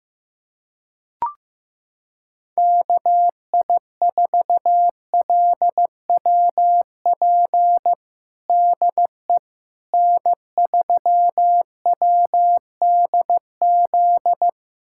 Morse code, heard as a keyed tone, about 15 wpm, 700 Hz. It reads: KI4LWP DE N3WDZ
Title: USA Callsign DE Callsign 15wpm